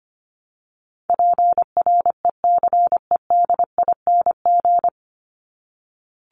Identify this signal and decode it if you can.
Morse code, heard as a keyed tone, about 25 wpm, 700 Hz. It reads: PRECEDING
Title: ING Suffix - Top 100 Words with Base Words 25wpm